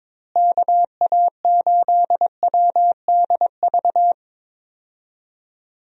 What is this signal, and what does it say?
Morse code, heard as a keyed tone, about 21 wpm, 700 Hz. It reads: KA8WDV
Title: USA Call Signs - 6 Characters - Repeated 3x - 22wpm